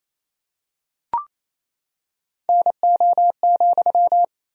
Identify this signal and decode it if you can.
Morse code, heard as a keyed tone, about 28 wpm, 700 Hz. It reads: NO,
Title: 2x Word-Spacing - Sentences from Top 300 Words - 28wpm